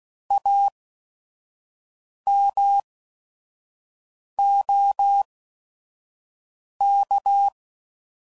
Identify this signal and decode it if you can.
Morse code, heard as a keyed tone, about 16 wpm, spaced as for 5 wpm, 800 Hz. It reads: AMOK
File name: Prefixes / By First Letter / As